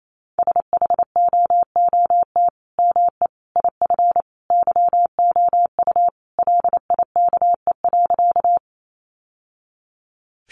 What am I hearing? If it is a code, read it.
Morse code, heard as a keyed tone, about 28 wpm, 700 Hz. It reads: SHOOT ME IF YOU LIKE.
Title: Sentences from Top 1200 Words 28wpm